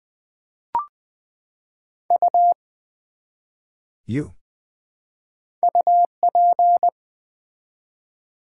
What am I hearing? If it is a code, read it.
Morse code, heard as a keyed tone, about 20 wpm, 700 Hz. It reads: U UP
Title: Build A Word - 1 Letter At A Time - 3 to 5-Letter Words - 20wpm